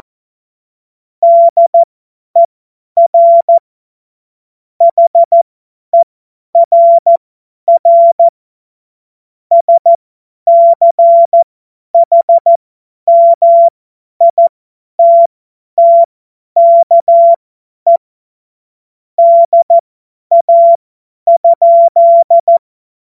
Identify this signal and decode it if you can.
Morse code, heard as a keyed tone, about 14 wpm, spaced as for 10 wpm, 680 Hz. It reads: DER HERR SCHMITTKE DA?
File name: Morsekurs / Witze / Witz 25. April 2023 50 F70